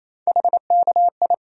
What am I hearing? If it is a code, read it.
Morse code, heard as a keyed tone, about 28 wpm, 700 Hz. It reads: HKI
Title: USA Call Sign Postfixes - 3 Letters 28wpm